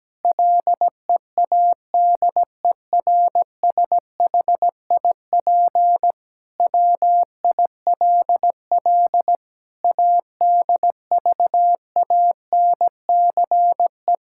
Morse code, heard as a keyed tone, about 17 wpm, 700 Hz. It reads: LEADERSHIP WILL ADVANCE
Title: Sentences from Top 1600 Words 17wpm